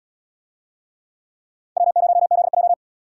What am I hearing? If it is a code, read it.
Morse code, heard as a keyed tone, about 67 wpm, 700 Hz. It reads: W0ZJ